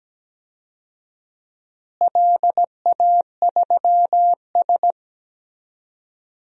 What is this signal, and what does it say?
Morse code, heard as a keyed tone, about 17 wpm, 700 Hz. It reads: LA3S
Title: DXCC-100 Call Signs - 4 Characters - Repeated 2x - 17wpm